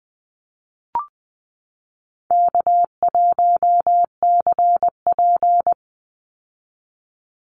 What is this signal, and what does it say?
Morse code, heard as a keyed tone, about 20 wpm, 700 Hz. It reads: K1CP
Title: Easy Call Signs Repeated 3x - 20wpm